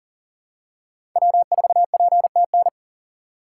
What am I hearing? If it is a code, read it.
Morse code, heard as a keyed tone, about 40 wpm, 700 Hz. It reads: W4PTN
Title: USA Call Signs - 5 Characters - 6x Character Spacing - 40wpm